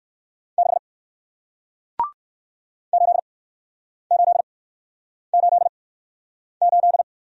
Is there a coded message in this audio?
Yes: Morse code from 0.6 to 7.0 s, heard as a keyed tone, about 58 wpm, spaced as for 9 wpm, 700 Hz. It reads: 6 8888